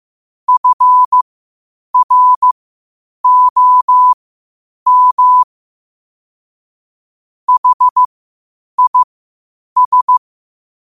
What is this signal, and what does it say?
Morse code, heard as a keyed tone, about 15 wpm, spaced as for 8 wpm, 1000 Hz. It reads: FROM HIS
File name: qotdfiles / 8wpm